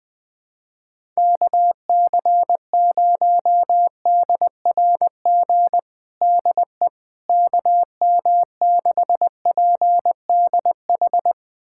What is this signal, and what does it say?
Morse code, heard as a keyed tone, about 20 wpm, 700 Hz. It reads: KC0DRG DE KM6PDH
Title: USA Callsign DE Callsign - 6 Characters - 20wpm